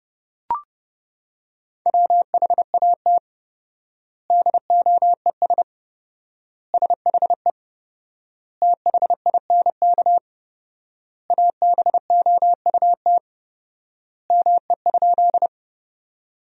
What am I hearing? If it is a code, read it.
Morse code, heard as a keyed tone, about 31 wpm, 700 Hz. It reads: WHAT DOES SHE THINK ABOUT ME?
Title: 4x Word-Spacing - Sentences from Top 200 Words - 30wpm